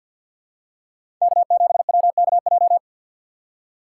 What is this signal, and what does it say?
Morse code, heard as a keyed tone, about 49 wpm, 700 Hz. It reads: K7WKJ
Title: USA Call Signs - 2x Character Spacing - 50wpm